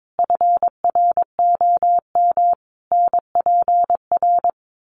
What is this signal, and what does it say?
Morse code, heard as a keyed tone, about 22 wpm, 700 Hz. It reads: FROM NPR
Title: Sets of 5 Words Encore 22wpm